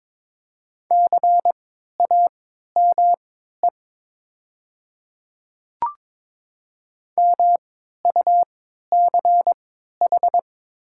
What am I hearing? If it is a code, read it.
Morse code, heard as a keyed tone, about 22 wpm, spaced as for 13 wpm, 700 Hz. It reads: CAME MUCH
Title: 3x Character Spacing - Top 200 Words - 22wpm